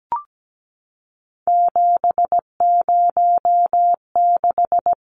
Morse code, heard as a keyed tone, about 17 wpm, 700 Hz. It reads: 706
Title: Numbers - 3-Digit 17wpm